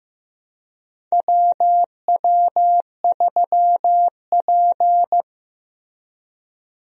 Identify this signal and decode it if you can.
Morse code, heard as a keyed tone, about 15 wpm, 700 Hz. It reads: WW3P
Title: USA Call Signs - 4 Characters - Repeated 3x - 15wpm